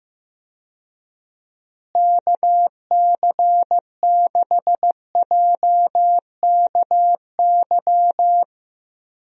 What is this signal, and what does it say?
Morse code, heard as a keyed tone, about 15 wpm, 700 Hz. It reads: KC6JKY
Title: USA Call Signs - 6 Characters - Repeated 2x - 15wpm